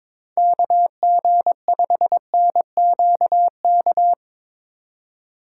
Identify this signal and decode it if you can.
Morse code, heard as a keyed tone, about 22 wpm, 700 Hz. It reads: KG5NQK
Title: USA Call Signs - 6 Characters - 6x Character Spacing - 22wpm